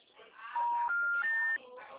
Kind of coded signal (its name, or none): telephone call progress tone